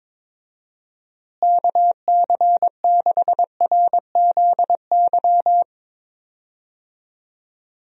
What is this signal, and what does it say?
Morse code, heard as a keyed tone, about 22 wpm, 700 Hz. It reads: KC6RZY
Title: USA Call Signs - 6 Characters - Repeated 3x - 22wpm